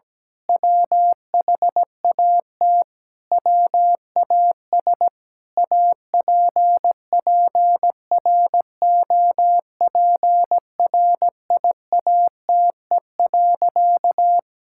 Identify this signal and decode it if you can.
Morse code, heard as a keyed tone, about 17 wpm, 700 Hz. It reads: WHAT WAS APPROPRIATE.
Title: Sentences from Top 1000 Words - Review Part 5 - 17wpm